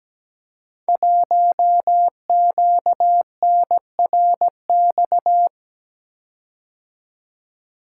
Morse code, heard as a keyed tone, about 17 wpm, 700 Hz. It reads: 1QNRX
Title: Code Groups - 5 Letter:Numbers - 17wpm